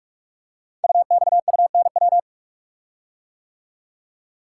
Morse code, heard as a keyed tone, about 44 wpm, 700 Hz. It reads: UXUNW
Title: Code Groups - 5 Letters - 45wpm